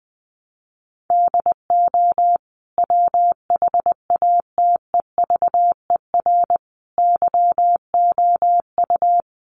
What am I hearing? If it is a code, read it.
Morse code, heard as a keyed tone, about 20 wpm, 700 Hz. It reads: DO WHATEVER YOU
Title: Sentences from Top 1100 Words 20wpm